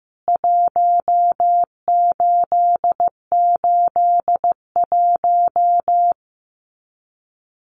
Morse code, heard as a keyed tone, about 14 wpm, 700 Hz. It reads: 1881
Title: CWT Exchange 15wpm